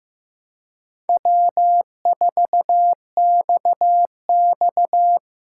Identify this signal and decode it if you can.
Morse code, heard as a keyed tone, about 15 wpm, 700 Hz. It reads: W4XX
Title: USA Call Signs - 4 Characters - 3x Character Spacing - 15wpm